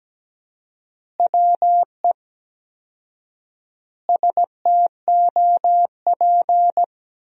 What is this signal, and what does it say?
Morse code, heard as a keyed tone, about 17 wpm, 700 Hz. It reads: WE STOP
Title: 4x Word-Spacing - Sentences from Top 300 Words - 17wpm